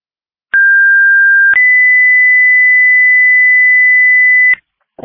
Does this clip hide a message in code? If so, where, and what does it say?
Yes: two-tone page from 0.5 to 4.5 s, heard as a one-second tone, then a three-second tone, about 1600 then 2000 Hz, with noise.